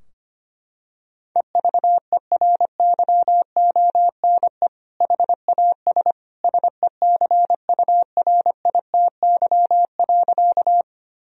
Morse code, heard as a keyed tone, about 25 wpm, 700 Hz. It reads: EVERYONE HAS SECURITY.
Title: Sentences from Top 1000 Words - Review Part 4 - 25wpm